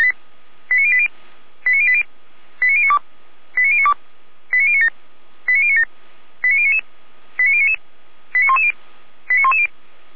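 Five-tone selective call (ZVEI), heard as quick runs of five tones, tones of about 70 ms. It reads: 89080 89080 89092 89092 89097 89097 89099 89099 89110 89110
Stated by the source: Five-tone ZVEI selective call 89073 received on SDR by feuerwehr-orlamuende fire brigade.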